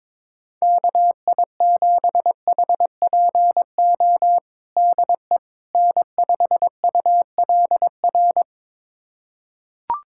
Morse code, heard as a keyed tone, about 22 wpm, 700 Hz. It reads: KI7HPO DE N5ULR
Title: USA Callsign DE Callsign 22wpm